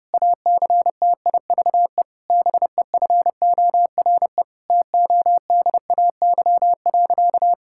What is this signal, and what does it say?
Morse code, heard as a keyed tone, about 30 wpm, 700 Hz. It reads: ACTIVE BEFORE TODAY.
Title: Sentences from Top 1000 Words - Review Part 6 - 30wpm